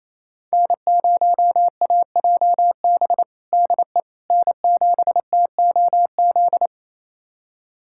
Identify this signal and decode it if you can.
Morse code, heard as a keyed tone, about 28 wpm, 700 Hz. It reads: N0AJB DE N7TOZ
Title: USA Callsign DE Callsign - 5 Characters - 28wpm